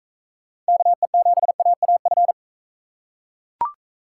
Morse code, heard as a keyed tone, about 40 wpm, 700 Hz. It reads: KE7AAF